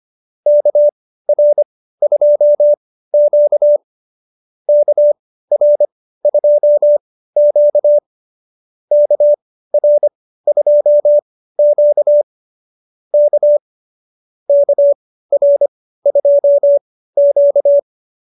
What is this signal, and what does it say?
Morse code, heard as a keyed tone, about 25 wpm, spaced as for 15 wpm, 590 Hz. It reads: KR2Q KR2Q KR2Q K KR2Q